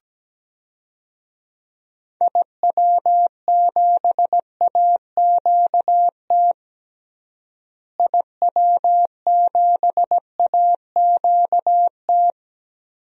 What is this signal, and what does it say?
Morse code, heard as a keyed tone, about 17 wpm, 700 Hz. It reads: IW7AQT IW7AQT
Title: DXCC-100 Call Signs - 6 Characters - Repeated 2x - 17wpm